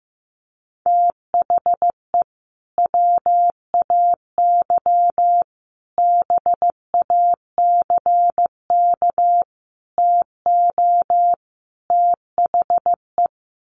Morse code, heard as a keyed tone, about 15 wpm, 700 Hz. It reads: THE WAY BACK TO THE
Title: Sets of 5 Words Encore2 15wpm